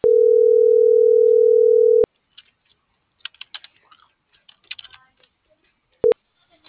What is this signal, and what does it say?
Ringback tone (North America), 440 + 480 Hz.